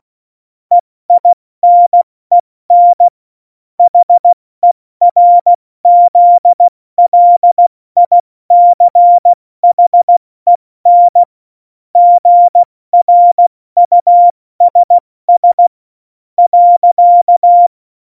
Morse code, heard as a keyed tone, about 16 wpm, spaced as for 14 wpm, 700 Hz. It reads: EINEN HERZLICHEN GRUSS.
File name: Morsekurs / Witze / Witz 18. Februar 2020, 70 F80